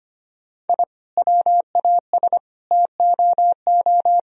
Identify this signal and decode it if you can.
Morse code, heard as a keyed tone, about 25 wpm, 700 Hz. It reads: I WAS TOO
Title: Sentences from Top 1000 Words - Review Part 4 - 25wpm